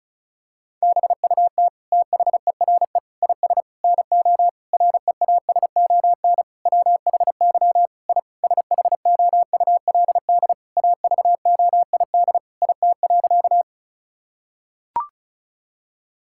Morse code, heard as a keyed tone, about 36 wpm, 700 Hz. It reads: BUT THERE IS NO REASON WHY I SHOULD AVOID IT.
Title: Sentences from Top 1000 Words - Review Part 6 - 35wpm